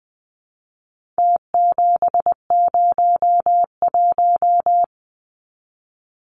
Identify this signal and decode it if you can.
Morse code, heard as a keyed tone, about 20 wpm, 700 Hz. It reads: T701
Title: Numbers - Contest Style - 4-Digit 20wpm